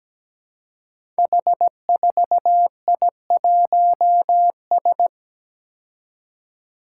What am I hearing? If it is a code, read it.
Morse code, heard as a keyed tone, about 17 wpm, 700 Hz. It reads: H4I1S
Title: Five Challenging Letter-Number Combinations 17wpm